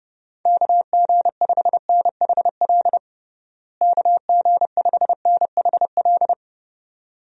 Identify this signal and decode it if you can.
Morse code, heard as a keyed tone, about 30 wpm, 700 Hz. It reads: KG5NHL KG5NHL
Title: USA Call Signs - 6 Characters - Repeated 2x - 30wpm